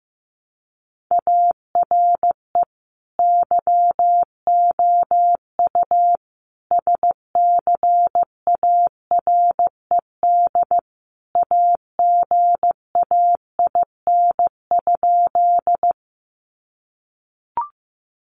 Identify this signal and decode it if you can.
Morse code, heard as a keyed tone, about 15 wpm, 700 Hz. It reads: ARE YOU SCARED AGAIN?
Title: Sentences from Top 700 Words 15wpm